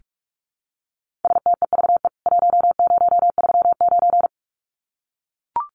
Morse code, heard as a keyed tone, about 44 wpm, 700 Hz. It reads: STEVE 1039
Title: CWT Exchange 45wpm